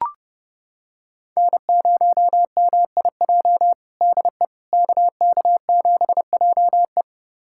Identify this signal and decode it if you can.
Morse code, heard as a keyed tone, about 32 wpm, 700 Hz. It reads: N0MIJ DE KK7JE
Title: USA Callsign DE Callsign - 5 Characters - 30wpm